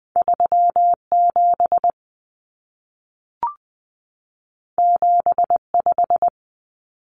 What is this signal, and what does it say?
Morse code, heard as a keyed tone, about 20 wpm, 700 Hz. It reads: 37 75
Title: Common Cut Numbers - 2-Digit 20wpm